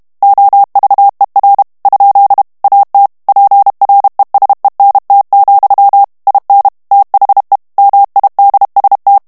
Morse code, heard as a keyed tone, about 32 wpm, 800 Hz. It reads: OVER ? AT PRESENT, IN THE MIDST
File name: morse mp3 / ANGLAIS texte clair 1 vitesse 30 mots minute